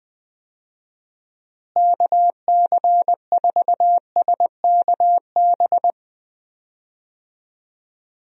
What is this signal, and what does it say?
Morse code, heard as a keyed tone, about 20 wpm, 700 Hz. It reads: KC4SKB
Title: USA Call Signs - 6 Characters - Repeated 2x - 20wpm